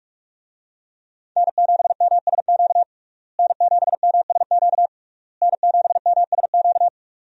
Morse code, heard as a keyed tone, about 43 wpm, 700 Hz. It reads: N7MSQ N7MSQ N7MSQ
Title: USA Call Signs - 5 Characters - Repeated 3x - 45wpm